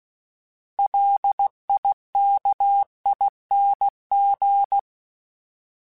Morse code, heard as a keyed tone, about 16 wpm, 800 Hz. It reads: LIKING
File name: Alphabetical / Alphabetical mp3s / Ls @15